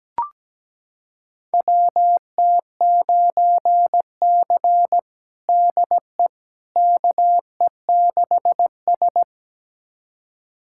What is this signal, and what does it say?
Morse code, heard as a keyed tone, about 17 wpm, 700 Hz. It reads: WT9C DE KE6S
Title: USA Callsign DE Callsign - 4 Characters - 17wpm